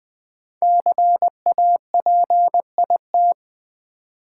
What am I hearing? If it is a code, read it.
Morse code, heard as a keyed tone, about 20 wpm, 700 Hz. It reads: CAPIT
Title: Build A Word - 5 Letters At A Time - 10-Letter Words - 20wpm